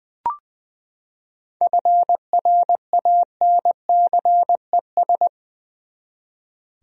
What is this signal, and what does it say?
Morse code, heard as a keyed tone, about 20 wpm, 700 Hz. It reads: FRANCES
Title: USA Common Names - 20wpm